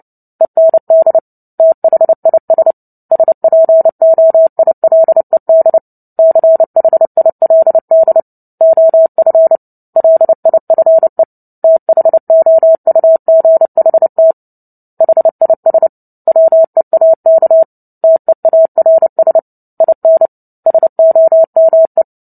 Morse code, heard as a keyed tone, about 30 wpm, 650 Hz. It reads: END THIS SPOILED CHILD OF LIFE THOUGHT HIS WEAK TEARS IN SOME
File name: Book Two / 03-Chapter 3 - 29 WPM